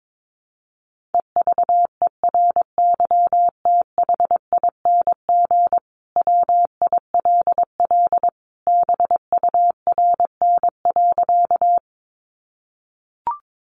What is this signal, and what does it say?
Morse code, heard as a keyed tone, about 22 wpm, 700 Hz. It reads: EVERYTHING WILL BURN.
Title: Sentences from Top 1300 Words 22wpm